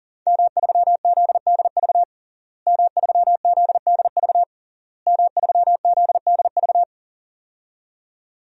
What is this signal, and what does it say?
Morse code, heard as a keyed tone, about 40 wpm, 700 Hz. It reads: M3ZDV M3ZDV M3ZDV
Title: DXCC-100 Call Signs - 5 Characters - Repeated 3x - 40wpm